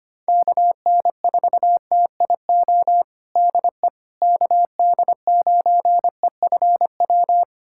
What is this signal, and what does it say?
Morse code, heard as a keyed tone, about 25 wpm, 700 Hz. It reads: KN4TIO DE KD9EFW
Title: USA Callsign DE Callsign - 6 Characters - 25wpm